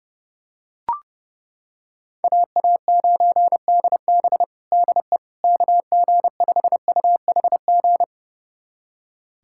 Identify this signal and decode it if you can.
Morse code, heard as a keyed tone, about 30 wpm, 690 Hz. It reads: AA9DB DE KG5UHG
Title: USA Callsign DE Callsign 30wpm